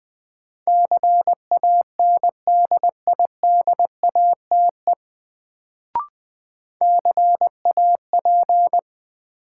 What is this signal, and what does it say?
Morse code, heard as a keyed tone, about 20 wpm, 700 Hz. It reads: CANDIDATE CAP
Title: Sentences from Top 1600 Words - New Word Drill - 20wpm